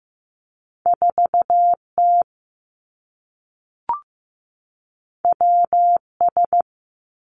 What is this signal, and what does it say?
Morse code, heard as a keyed tone, about 15 wpm, 700 Hz. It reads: 4 T W S
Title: Two Challenging Letter-Number Combinations 15wpm